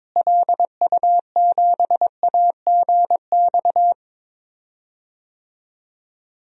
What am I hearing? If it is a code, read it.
Morse code, heard as a keyed tone, about 21 wpm, 700 Hz. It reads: LU7AGX
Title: DXCC-100 Call Signs - 6 Characters - Repeated 3x - 22wpm